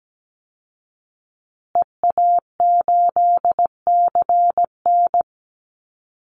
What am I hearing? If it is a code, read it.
Morse code, heard as a keyed tone, about 17 wpm, 700 Hz. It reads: EA8CN
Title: Call Signs Repeated 2x - 17wpm